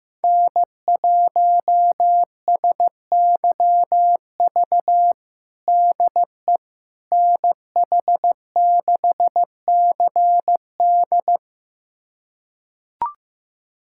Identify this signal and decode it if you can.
Morse code, heard as a keyed tone, about 15 wpm, 700 Hz. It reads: N1SYV DE NH6CD
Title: USA Callsign DE Callsign - 5 Characters - 15wpm